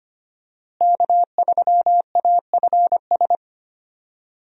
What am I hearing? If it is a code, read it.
Morse code, heard as a keyed tone, about 25 wpm, 700 Hz. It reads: K3AFS
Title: USA Call Signs - 5 Characters - Repeated 2x - 25wpm